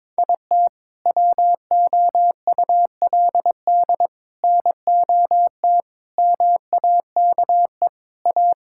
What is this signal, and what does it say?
Morse code, heard as a keyed tone, about 21 wpm, 700 Hz. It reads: IT WOULD NOT MAKE A
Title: Sentences from Top 1000 Words - Review Part 5 - 22wpm